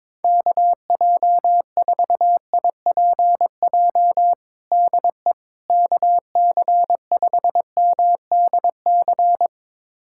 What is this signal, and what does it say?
Morse code, heard as a keyed tone, about 22 wpm, 700 Hz. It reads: KJ4IPJ DE KC5MDC
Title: USA Callsign DE Callsign - 6 Characters - 22wpm